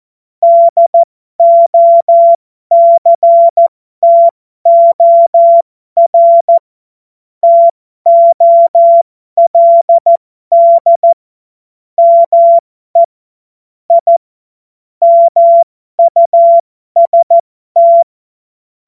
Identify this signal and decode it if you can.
Morse code, heard as a keyed tone, about 14 wpm, spaced as for 12 wpm, 680 Hz. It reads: DOCTOR TOLD ME I MUST
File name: Morsekurs / Witze